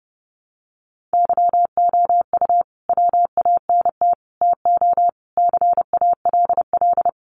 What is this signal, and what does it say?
Morse code, heard as a keyed tone, about 30 wpm, 700 Hz. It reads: YOU WANT TO CALL
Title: Sets of 5 Words Encore2 30wpm